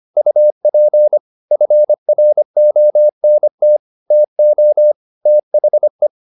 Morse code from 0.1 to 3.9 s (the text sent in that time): UP FRONT